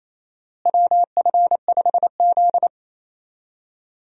Morse code, heard as a keyed tone, about 27 wpm, 700 Hz. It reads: WF5Z
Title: USA Call Signs - 4 Characters - Repeated 3x - 28wpm